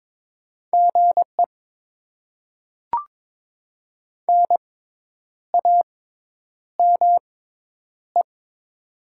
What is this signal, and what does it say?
Morse code, heard as a keyed tone, about 22 wpm, spaced as for 9 wpm, 700 Hz. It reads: GE NAME
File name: Top 20 QSO Elements - 6x Character Spacing - 22wpm